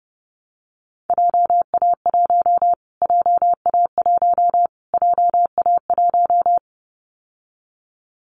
Morse code, heard as a keyed tone, about 30 wpm, 700 Hz. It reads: JA1 JA1 JA1
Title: Callsign Prefixes for Top 25 DXCC - Repeated 3x - 30wpm